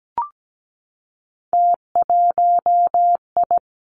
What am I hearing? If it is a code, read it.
Morse code, heard as a keyed tone, about 17 wpm, 700 Hz. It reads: T1I
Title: Three Challenging Letter-Number Combinations 17wpm